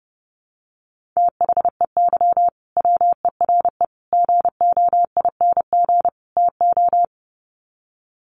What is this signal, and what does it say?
Morse code, heard as a keyed tone, about 28 wpm, 700 Hz. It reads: THEY WERE GOING TO
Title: Sets of 4 Words 30wpm